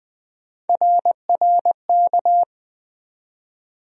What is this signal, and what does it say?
Morse code, heard as a keyed tone, about 20 wpm, 700 Hz. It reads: RRK
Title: USA Call Sign Postfixes - 3 Letters 20wpm